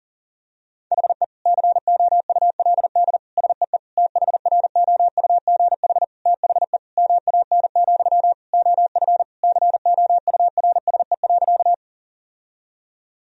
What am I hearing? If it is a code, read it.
Morse code, heard as a keyed tone, about 40 wpm, 700 Hz. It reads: HE COULD SEE THROUGH THE MAN, OF COURSE.
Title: Sentences from Top 1000 Words - Review Part 1 - 40wpm